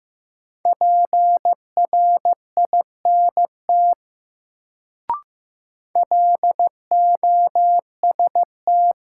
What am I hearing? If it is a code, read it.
Morse code, heard as a keyed tone, about 15 wpm, 700 Hz. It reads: PRINT LOST
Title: Sentences from Top 1400 Words - New Word Drill - 15wpm